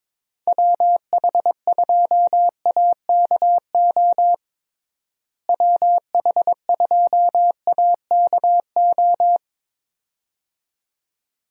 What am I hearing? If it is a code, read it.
Morse code, heard as a keyed tone, about 22 wpm, 700 Hz. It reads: WH2AKO WH2AKO
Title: USA Call Signs - 6 Characters - Repeated 2x - 22wpm